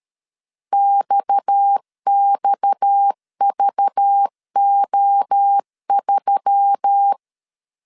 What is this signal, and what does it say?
Morse code, heard as a keyed tone, about 13 wpm, 800 Hz. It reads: XXVO3
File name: CW / DET / 1 6 1500C S B 12wpm 3431